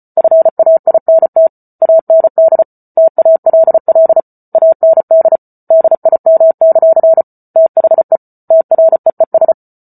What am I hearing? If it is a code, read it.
Morse code, heard as a keyed tone, about 34 wpm, 650 Hz. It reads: FAINT AND TALL AND DIM; THE TREES